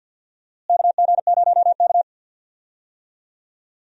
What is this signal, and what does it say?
Morse code, heard as a keyed tone, about 52 wpm, 700 Hz. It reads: KG0K
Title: USA Call Signs - 4 Characters - Repeated 2x - 50wpm